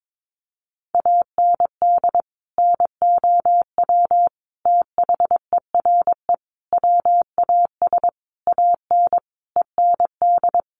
Morse code, heard as a keyed tone, about 22 wpm, 700 Hz. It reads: AND NOW THERE WAS AN END
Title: Sentences from Top 1400 Words 22wpm